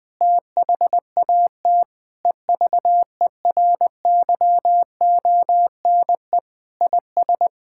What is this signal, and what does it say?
Morse code, heard as a keyed tone, about 20 wpm, 700 Hz. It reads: THAT EVERYONE IS
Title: Sentences with 8-Word Phrases - 20wpm